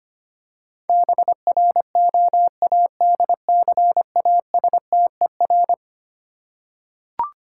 Morse code, heard as a keyed tone, about 25 wpm, 700 Hz. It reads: BROADCASTER
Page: morse-code-ninja-practice-25wpm